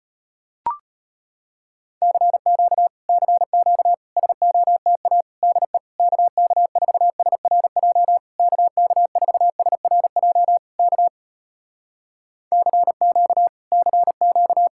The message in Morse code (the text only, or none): CQ CQ SOTA DE KK4SRJ KK4SRJ K CQ CQ